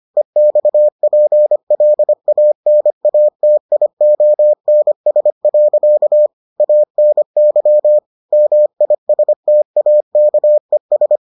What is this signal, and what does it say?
Morse code, heard as a keyed tone, about 25 wpm, 600 Hz. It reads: EXPLANATIONS. ANY MISTAKES